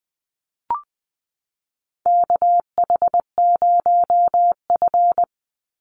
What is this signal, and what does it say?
Morse code, heard as a keyed tone, about 20 wpm, 700 Hz. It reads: KH0F
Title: USA Call Signs 20wpm